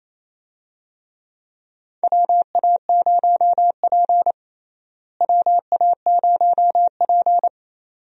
Morse code, heard as a keyed tone, about 27 wpm, 700 Hz. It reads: WA0P WA0P
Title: USA Call Signs - 4 Characters - Repeated 2x - 28wpm